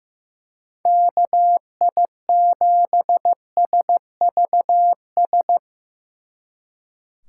Morse code, heard as a keyed tone, about 17 wpm, 700 Hz. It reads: KI7SVS
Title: Speed Racing - USA 2x3 Call Signs 15wpm